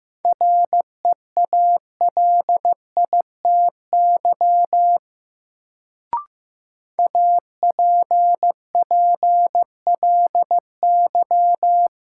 Morse code, heard as a keyed tone, about 15 wpm, 700 Hz. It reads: REALITY APPLY
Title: Sentences from Top 1200 Words - New Word Drill - 15wpm